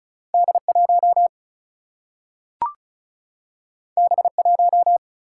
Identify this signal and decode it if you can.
Morse code, heard as a keyed tone, about 36 wpm, 700 Hz. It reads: D 1 B 1
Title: Single-Letter Single-Number 35wpm